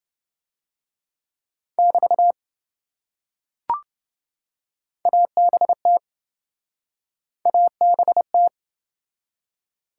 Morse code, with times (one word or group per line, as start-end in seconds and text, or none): 1.8-2.3 <BT>
5.1-6.0 ABT
7.5-8.5 ABT